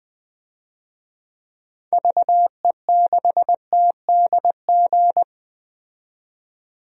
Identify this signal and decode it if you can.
Morse code, heard as a keyed tone, about 20 wpm, 700 Hz. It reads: VE6TDG
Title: DXCC-100 Call Signs - 6 Characters - 20wpm